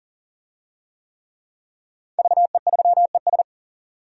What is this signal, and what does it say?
Morse code, heard as a keyed tone, about 41 wpm, 700 Hz. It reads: VE3ES
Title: DXCC-100 Call Signs - 5 Characters - 40wpm